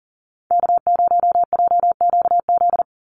Morse code, heard as a keyed tone, about 40 wpm, 700 Hz. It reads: K0JQZ
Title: NA SOTA Activator Callsigns 40wpm